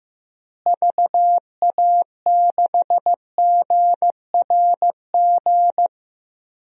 Morse code, heard as a keyed tone, about 15 wpm, 700 Hz. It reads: VA6GRG
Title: DXCC-100 Call Signs - 6 Characters - Repeated 3x - 15wpm